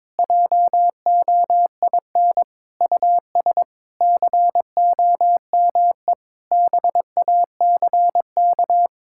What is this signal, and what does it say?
Morse code, heard as a keyed tone, about 22 wpm, 700 Hz. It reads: JOIN US COME BACK
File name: practice / 22wpm / Sentences with 4-Word Phrases - 22wpm